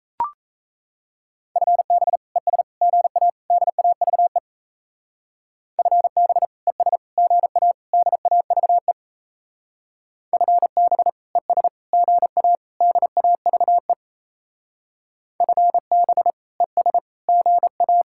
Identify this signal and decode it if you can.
Morse code, heard as a keyed tone, about 35 wpm, 700 Hz. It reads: FB ES GA DAVE FB ES GA DAVE FB ES GA DAVE FB ES GA
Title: Speed-Racing Common QSO Phrases - 28wpm